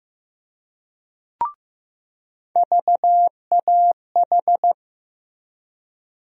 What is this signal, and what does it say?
Morse code, heard as a keyed tone, about 15 wpm, 700 Hz. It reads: VAH
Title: USA Call Sign Postfixes - 3 Letters 15wpm